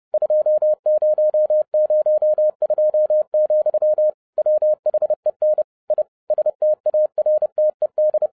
Morse code, heard as a keyed tone, about 30 wpm, 610 Hz. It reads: 2002, WHEN I STARTED
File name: 30wpm